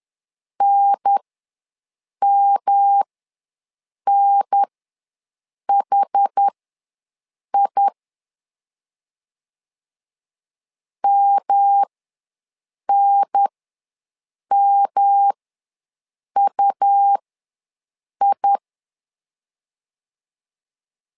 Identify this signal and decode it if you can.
Morse code, heard as a keyed tone, about 10 wpm, spaced as for 6 wpm, 800 Hz. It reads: NMNHI MNMUI
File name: LEZIONI / 3 MWRO / 3 M ws 10wpm 2121